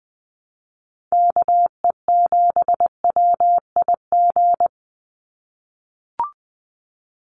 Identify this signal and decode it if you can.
Morse code, heard as a keyed tone, about 20 wpm, 700 Hz. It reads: KE7WIG
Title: USA Call Signs 20wpm